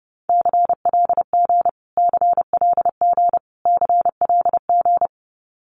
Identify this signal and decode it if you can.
Morse code, heard as a keyed tone, about 30 wpm, 700 Hz. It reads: CLG CLG CLG
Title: Abbreviations Repeated 3x - 30wpm